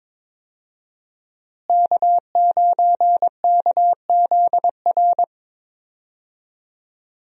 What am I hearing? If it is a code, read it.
Morse code, heard as a keyed tone, about 21 wpm, 700 Hz. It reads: K9KZR
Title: USA Call Signs - 5 Characters - Repeated 2x - 22wpm